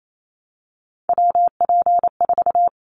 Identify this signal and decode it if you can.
Morse code, heard as a keyed tone, about 28 wpm, 700 Hz. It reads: WP4
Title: Callsign Prefixes for Top 25 DXCC - 28wpm